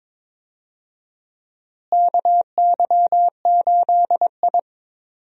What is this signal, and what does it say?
Morse code, heard as a keyed tone, about 22 wpm, 700 Hz. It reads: KY8I
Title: USA Call Signs - 4 Characters - Repeated 3x - 22wpm